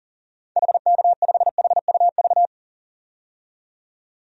Morse code, heard as a keyed tone, about 40 wpm, 700 Hz. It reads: HK5HUV
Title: DXCC-100 Call Signs - 6 Characters - 40wpm